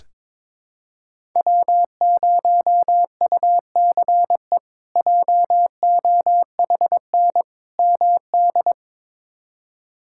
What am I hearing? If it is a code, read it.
Morse code, heard as a keyed tone, about 22 wpm, 700 Hz. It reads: W0UCE JOHN MD
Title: K1USN SST QSO Practice - 22wpm